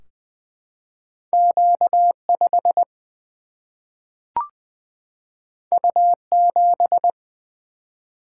Morse code, heard as a keyed tone, about 20 wpm, 700 Hz. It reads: Q 5 U 7
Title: Single-Letter Single-Number 20wpm